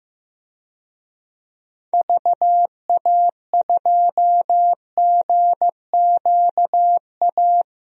Morse code, heard as a keyed tone, about 15 wpm, 700 Hz. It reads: VA2GQA
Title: DXCC-100 Call Signs - 6 Characters - 15wpm